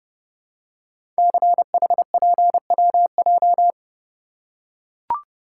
Morse code, heard as a keyed tone, about 30 wpm, 700 Hz. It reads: CHPWJ
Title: Code Groups - 5 Letters - 30wpm